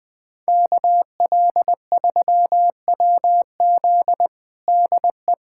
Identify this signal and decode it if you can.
Morse code, heard as a keyed tone, about 20 wpm, 700 Hz. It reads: KL3WZ DE
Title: USA Callsign DE Callsign - 5 Characters - 20wpm